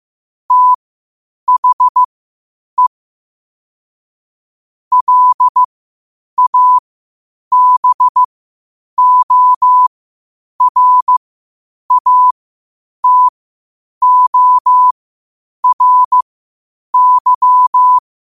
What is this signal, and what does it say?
Morse code, heard as a keyed tone, about 15 wpm, spaced as for 8 wpm, 1000 Hz. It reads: THE LABORATORY
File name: qotdfiles / 8wpm